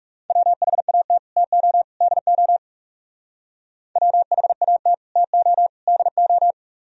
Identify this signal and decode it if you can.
Morse code, heard as a keyed tone, about 38 wpm, 700 Hz. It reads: WHAT TO DO WHAT TO DO
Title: Speed-Racing - Sets of 3 Words 30wpm